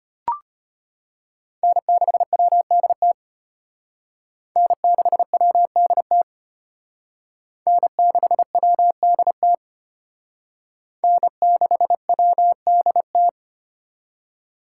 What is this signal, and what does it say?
Morse code, heard as a keyed tone, about 32 wpm, 700 Hz. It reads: N6WDT N6WDT N6WDT N6WDT